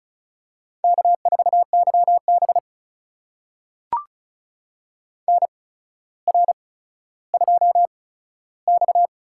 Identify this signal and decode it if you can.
Morse code, heard as a keyed tone, about 36 wpm, spaced as for 15 wpm, 700 Hz. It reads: K4YB NR2X
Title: USA Call Signs - 4 Characters - 8x Character Spacing - 35wpm